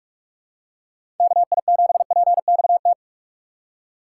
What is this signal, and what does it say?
Morse code, heard as a keyed tone, about 46 wpm, 700 Hz. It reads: KI7PXT
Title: USA Call Signs - 6 Characters - 6x Character Spacing - 45wpm